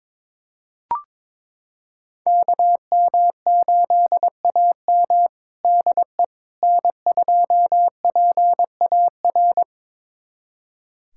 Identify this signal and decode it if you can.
Morse code, heard as a keyed tone, about 21 wpm, 700 Hz. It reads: KM8AM DE N2PAR
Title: USA Callsign DE Callsign - 5 Characters - 22wpm